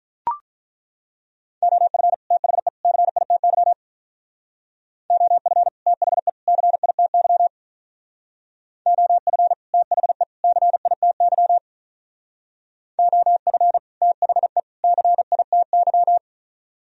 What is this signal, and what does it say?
Morse code, heard as a keyed tone, about 43 wpm, 700 Hz. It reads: OF THE CITY OF THE CITY OF THE CITY OF THE CITY
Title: Speed-Racing - Sets of 3 Words 35wpm